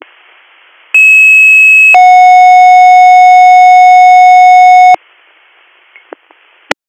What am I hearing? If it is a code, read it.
Two-tone page, heard as a one-second tone, then a three-second tone, about 2600 then 730 Hz. It is heard from 0.9 to 5.0 s.